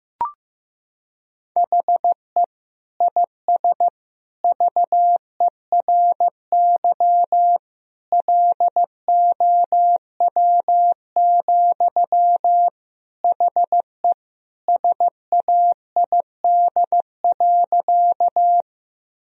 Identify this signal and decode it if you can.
Morse code, heard as a keyed tone, about 16 wpm, 700 Hz. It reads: HE IS VERY LOW, HE SAID.
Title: Sentences from Top 1000 Words - Review Part 1 - 15wpm